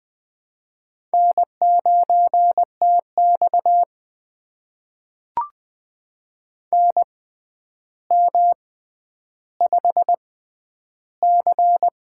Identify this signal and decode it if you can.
Morse code, heard as a keyed tone, about 20 wpm, spaced as for 10 wpm, 700 Hz. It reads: N9TX NM5C
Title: USA Call Signs - 4 Characters - 6x Character Spacing - 20wpm